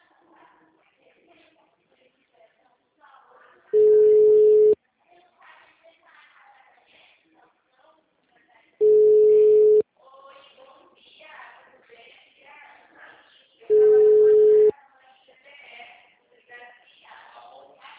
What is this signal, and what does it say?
Ringback tone (Europe, 425 Hz), with speech.
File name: out- -2001- - - .29091